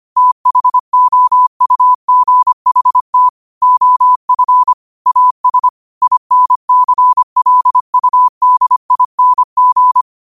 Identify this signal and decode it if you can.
Morse code, heard as a keyed tone, about 25 wpm, 1000 Hz. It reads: THOUGHT OF AS INCLUDING
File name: qotdfiles / 25wpm